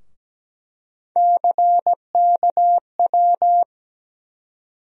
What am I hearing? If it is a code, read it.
Morse code, heard as a keyed tone, about 17 wpm, 700 Hz. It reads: CKW
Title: USA Call Sign Postfixes - 3 Letters 17wpm